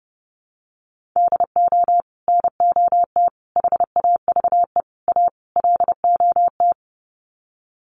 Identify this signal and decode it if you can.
Morse code, heard as a keyed tone, about 30 wpm, 700 Hz. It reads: DO NOT HAVE A LOT
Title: Sets of 5 Words Encore 30wpm